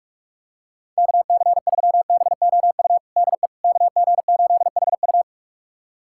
Morse code, heard as a keyed tone, about 43 wpm, 700 Hz. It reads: KC3BOU DE KG8HU
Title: USA Callsign DE Callsign 45wpm